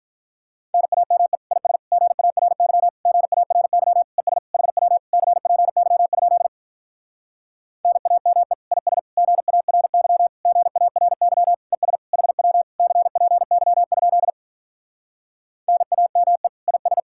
Morse code, heard as a keyed tone, about 48 wpm, 700 Hz. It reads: NAME IS GARY GARY ES HW CPY? NAME IS GARY GARY ES HW CPY? NAME IS